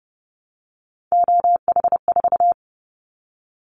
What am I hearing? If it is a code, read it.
Morse code, heard as a keyed tone, about 30 wpm, 700 Hz. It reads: OH4
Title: Callsign Prefixes for Top 25 DXCC - Repeated 3x - 30wpm